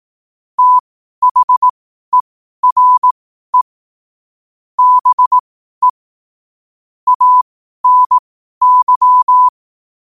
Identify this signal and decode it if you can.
Morse code, heard as a keyed tone, about 18 wpm, spaced as for 12 wpm, 1000 Hz. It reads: THERE BE ANY
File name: qotdfiles / 12wpm